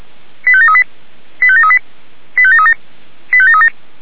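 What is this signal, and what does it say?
Five-tone selective call (ZVEI), heard as quick runs of five tones, tones of about 70 ms. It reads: 85738 85738 85737 85737
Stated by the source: ZVEI five-tone selective call received by SDR.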